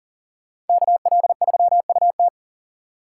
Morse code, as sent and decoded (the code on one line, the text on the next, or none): -.- .-.. ...-- ..- -
KL3UT